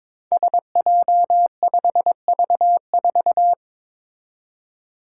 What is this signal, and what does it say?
Morse code, heard as a keyed tone, about 22 wpm, 700 Hz. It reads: SJ5V4
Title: Five Challenging Letter-Number Combinations 22wpm